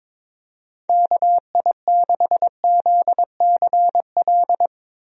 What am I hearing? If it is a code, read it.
Morse code, heard as a keyed tone, about 21 wpm, 700 Hz. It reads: KI6ZCL
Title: USA Call Signs - 6 Characters - Repeated 2x - 22wpm